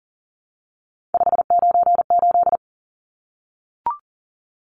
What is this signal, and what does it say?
Morse code, heard as a keyed tone, about 40 wpm, 700 Hz. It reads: 598
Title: Numbers - 3-Digit 40wpm